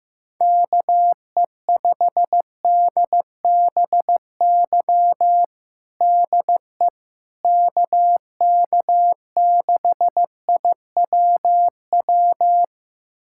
Morse code, heard as a keyed tone, about 16 wpm, 700 Hz. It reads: KE5DBY DE KK6IWW
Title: USA Callsign DE Callsign - 6 Characters - 15wpm